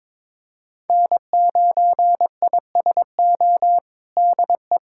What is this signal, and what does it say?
Morse code, heard as a keyed tone, about 22 wpm, 700 Hz. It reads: N9ISO DE
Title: USA Callsign DE Callsign - 5 Characters - 22wpm